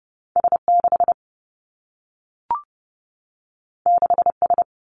Morse code, heard as a keyed tone, about 30 wpm, 700 Hz. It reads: S 6 6 S
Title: Two Challenging Letter-Number Combinations 30wpm